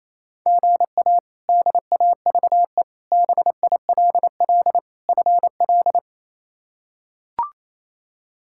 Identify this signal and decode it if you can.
Morse code, heard as a keyed tone, about 28 wpm, 700 Hz. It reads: GA DAVE BILL FL
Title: K1USN SST QSO Practice - 28wpm